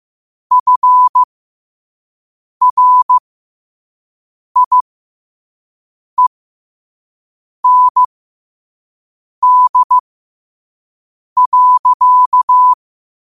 Morse code, heard as a keyed tone, about 15 wpm, spaced as for 5 wpm, 1000 Hz. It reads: FRIEND.
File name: qotdfiles / 5wpm